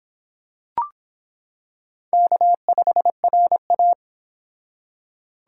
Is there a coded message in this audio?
Yes: Morse code from 2.1 to 3.9 s, heard as a keyed tone, about 25 wpm, 700 Hz. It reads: K5RA